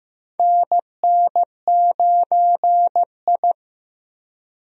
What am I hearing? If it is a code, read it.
Morse code, heard as a keyed tone, about 15 wpm, 700 Hz. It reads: NN9I
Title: USA Call Signs - 4 Characters - Repeated 2x - 15wpm